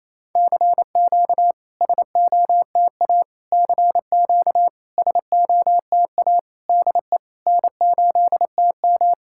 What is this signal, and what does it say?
Morse code, heard as a keyed tone, about 28 wpm, 700 Hz. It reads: CQ SOTA CQ SOTA DE N8TM
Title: SOTA QSO Practice 28wpm